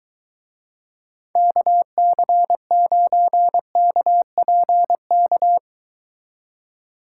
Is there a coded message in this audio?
Yes: Morse code from 1.4 to 5.6 s, heard as a keyed tone, about 23 wpm, 700 Hz. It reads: KC9KPK